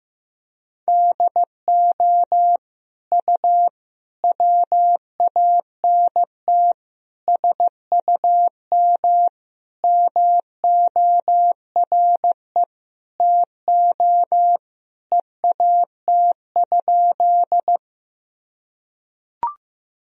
Morse code, heard as a keyed tone, about 15 wpm, 700 Hz. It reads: DO U WANT SUM MORE TO EAT?
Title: Sentences from Top 300 Words and Abbreviations 15wpm